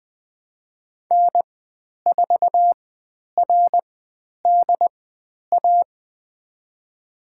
Morse code, heard as a keyed tone, about 20 wpm, spaced as for 10 wpm, 700 Hz. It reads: N4RDA
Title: 109 Lesson 18g - Letter m - Call Signs 20wpm